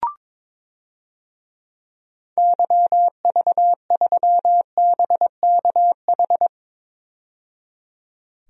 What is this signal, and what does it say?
Morse code, heard as a keyed tone, about 21 wpm, 700 Hz. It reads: YV3BKH
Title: DXCC-100 Call Signs - 6 Characters - 22wpm